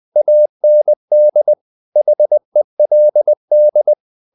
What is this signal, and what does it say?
Morse code, heard as a keyed tone, about 20 wpm, 600 Hz. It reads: AND HELD
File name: the old man and the sea / the old man and the sea part 1 20wpm